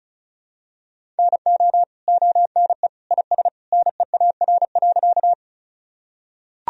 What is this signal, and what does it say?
Morse code, heard as a keyed tone, about 34 wpm, 700 Hz. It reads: NO ONE IS NEAR.
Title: Sentences from Top 1000 Words - Review Part 1 - 35wpm